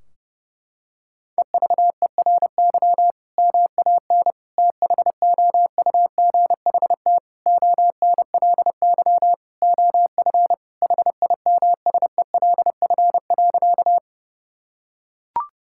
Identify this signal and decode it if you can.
Morse code, heard as a keyed tone, about 30 wpm, 700 Hz. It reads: EVERY MAN THOUGHT ONLY OF HIMSELF.
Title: Sentences from Top 1600 Words 30wpm